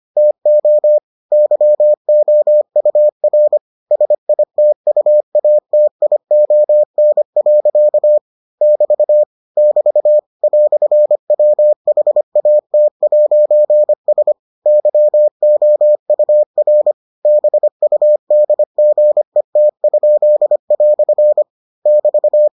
Morse code, heard as a keyed tone, about 25 wpm, 600 Hz. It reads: TO YOUR SITUATION. <BT> <BT> "WHAT'S YOUR BUDGET?" <BT>